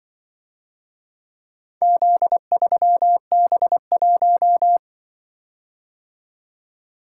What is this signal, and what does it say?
Morse code, heard as a keyed tone, about 24 wpm, 700 Hz. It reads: Z3B1
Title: Speed-Racing - Four Letter-Number Combinations - 20wpm